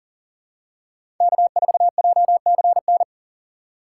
Morse code, heard as a keyed tone, about 40 wpm, 700 Hz. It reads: K4JCN
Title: USA Call Signs - 5 Characters - 3x Character Spacing - 40wpm